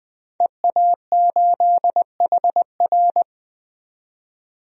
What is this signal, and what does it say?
Morse code, heard as a keyed tone, about 21 wpm, 700 Hz. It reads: EA8HR
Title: DXCC-100 Call Signs - 5 Characters - 20wpm